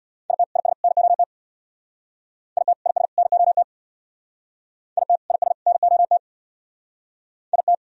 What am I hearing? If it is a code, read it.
Morse code, heard as a keyed tone, about 63 wpm, 700 Hz. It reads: IT IS NOT IT IS NOT IT IS NOT IT